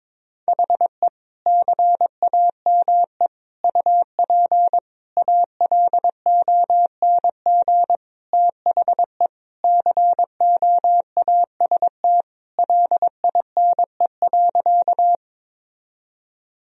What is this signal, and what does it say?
Morse code, heard as a keyed tone, about 21 wpm, 700 Hz. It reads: HE CAME UP ALONG THE COAST LINE.
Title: Sentences from Top 1000 Words - Review Part 4 - 22wpm